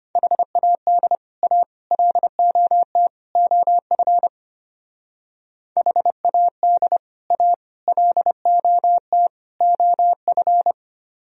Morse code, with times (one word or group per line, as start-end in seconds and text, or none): 0.2-1.2 HAD
1.4-1.6 A
1.9-3.1 LOT
3.4-4.3 OF
5.8-7.0 HAD
7.3-7.5 A
7.9-9.3 LOT
9.6-10.7 OF